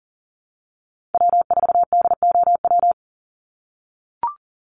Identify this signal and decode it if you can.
Morse code, heard as a keyed tone, about 40 wpm, 700 Hz. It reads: W4DOW
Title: NA SOTA Chaser Callsigns 40wpm